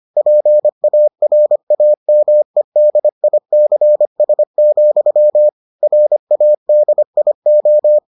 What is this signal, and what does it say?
Morse code, heard as a keyed tone, about 25 wpm, 600 Hz. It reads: PARAMEDICS, RADIO